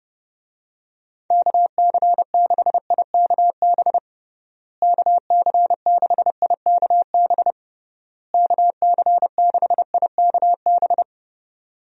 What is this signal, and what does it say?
Morse code, heard as a keyed tone, about 30 wpm, 700 Hz. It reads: KC6IKB KC6IKB KC6IKB
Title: USA Call Signs - 6 Characters - Repeated 3x - 30wpm